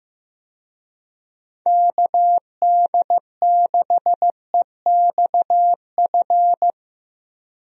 Morse code, heard as a keyed tone, about 15 wpm, 700 Hz. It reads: KD6EXF
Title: USA Call Signs - 6 Characters - Repeated 3x - 15wpm